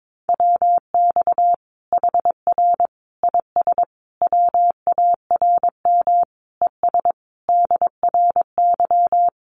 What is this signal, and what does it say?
Morse code, heard as a keyed tone, about 22 wpm, 700 Hz. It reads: WX HR IS WARM ES DRY
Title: Common QSO Phrases 22wpm